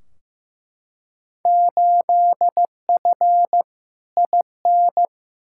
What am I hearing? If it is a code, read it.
Morse code, heard as a keyed tone, about 15 wpm, 700 Hz. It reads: 8F IN
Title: Field Day Exchanges Encore - 15wpm